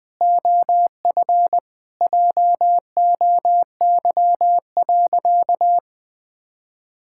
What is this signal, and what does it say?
Morse code, heard as a keyed tone, about 21 wpm, 700 Hz. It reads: OF JOY.
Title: Sentences with 2-Word Phrases - Spoken Beginning - 20wpm